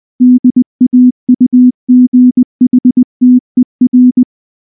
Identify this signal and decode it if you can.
Morse code, heard as a keyed tone, about 19 wpm, 250 Hz. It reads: DAUGHTER
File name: morse / morse-training-20WPM-5